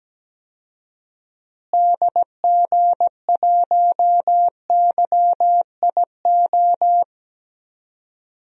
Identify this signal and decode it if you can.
Morse code, heard as a keyed tone, about 17 wpm, 700 Hz. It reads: DG1YIO
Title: DXCC-100 Call Signs - 6 Characters - Repeated 2x - 17wpm